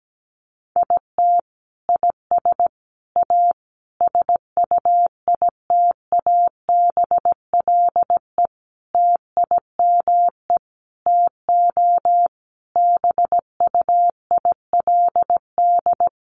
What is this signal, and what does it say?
Morse code, heard as a keyed tone, about 17 wpm, 700 Hz. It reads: IT IS A SUITABLE TIME TO BUILD
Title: Sentences from Top 1400 Words 17wpm